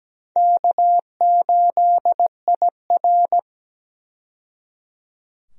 Morse code, heard as a keyed tone, about 17 wpm, 700 Hz. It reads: K8IR
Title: USA Call Signs - 4 Characters - 17wpm